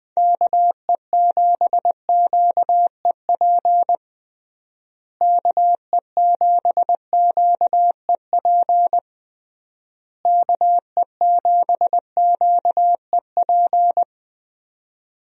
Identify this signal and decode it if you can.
Morse code, heard as a keyed tone, about 20 wpm, 700 Hz. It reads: KE7QEP KE7QEP KE7QEP
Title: USA Call Signs - 6 Characters - Repeated 3x - 20wpm